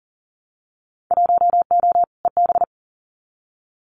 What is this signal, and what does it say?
Morse code, heard as a keyed tone, about 38 wpm, 700 Hz. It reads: 1O EB
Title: Winter Field Day Exchanges 40wpm